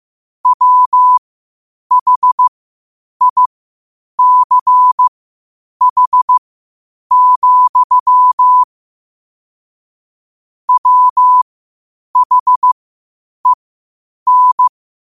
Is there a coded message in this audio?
Yes: Morse code from 0.5 to 14.7 s, heard as a keyed tone, about 15 wpm, spaced as for 8 wpm, 1000 Hz. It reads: WHICH, WHEN